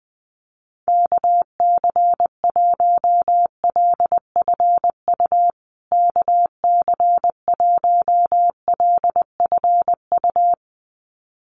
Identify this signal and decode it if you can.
Morse code, heard as a keyed tone, about 20 wpm, 700 Hz. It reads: KC1LFU KC1LFU
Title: USA Call Signs Repeated 2x 20wpm